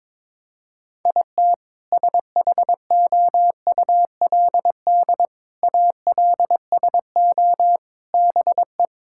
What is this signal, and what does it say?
Morse code, heard as a keyed tone, about 22 wpm, 700 Hz. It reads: IT SHOULD ALSO BE
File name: Sentences with 7-Word Phrases - 22wpm